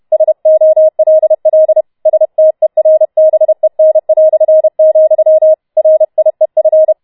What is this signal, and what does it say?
Morse code, heard as a keyed tone, about 31 wpm, 630 Hz. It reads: SOLL STERBEN", RIEF